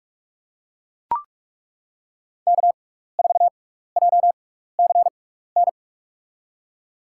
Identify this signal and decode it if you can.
Morse code, heard as a keyed tone, about 42 wpm, spaced as for 15 wpm, 700 Hz. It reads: K4JCN
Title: USA Call Signs - 5 Characters - 6x Character Spacing - 45wpm